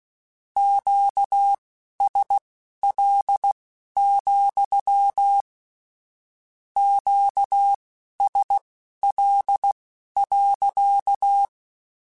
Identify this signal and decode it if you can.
Morse code, heard as a keyed tone, about 16 wpm, spaced as for 12 wpm, 800 Hz. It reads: QSL, QSL.
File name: Q Calls / Q Calls-mp3s / Q Calls sppc @15